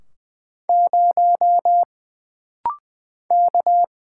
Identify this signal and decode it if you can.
Morse code, heard as a keyed tone, about 20 wpm, 700 Hz. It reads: 0 K
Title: Single Letter-Number - Rapid-Fire 20wpm